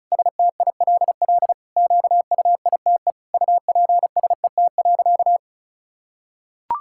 Morse code, heard as a keyed tone, about 36 wpm, 700 Hz. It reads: STILL QUITE UPSET.
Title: Sentences from Top 1000 Words - Review Part 5 - 35wpm